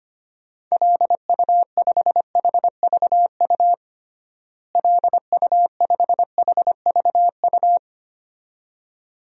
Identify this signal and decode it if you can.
Morse code, heard as a keyed tone, about 25 wpm, 700 Hz. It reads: LU5HVU LU5HVU
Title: DXCC-100 Call Signs - 6 Characters - Repeated 2x - 25wpm